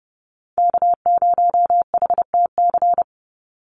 Morse code, heard as a keyed tone, about 30 wpm, 700 Hz. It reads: K0HTC
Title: USA Call Signs 30wpm